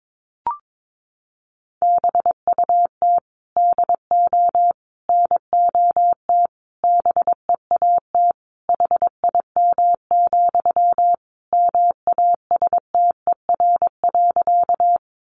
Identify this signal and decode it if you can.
Morse code, heard as a keyed tone, about 22 wpm, 700 Hz. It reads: BUT DO NOT BEAT HIM, MASTER.
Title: Sentences from Top 1300 Words 22wpm